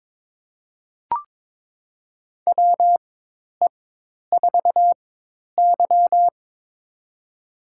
Morse code, heard as a keyed tone, about 22 wpm, spaced as for 10 wpm, 700 Hz. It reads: WE4Y
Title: USA Call Signs - 4 Characters - 4x Character Spacing - 22wpm